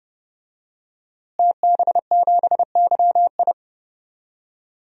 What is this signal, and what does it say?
Morse code, heard as a keyed tone, about 31 wpm, 700 Hz. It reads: TB7YI
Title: DXCC-100 Call Signs - 5 Characters - Repeated 2x - 30wpm